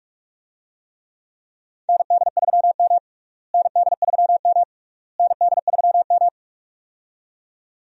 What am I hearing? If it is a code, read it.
Morse code, heard as a keyed tone, about 46 wpm, 700 Hz. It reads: ND3M ND3M ND3M
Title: USA Call Signs - 4 Characters - Repeated 3x - 45wpm